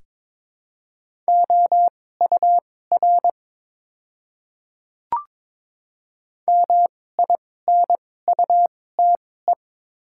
Morse code, heard as a keyed tone, about 22 wpm, spaced as for 16 wpm, 700 Hz. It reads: OUR MINUTE
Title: 2x Character Spacing - Top 500 Words - 22wpm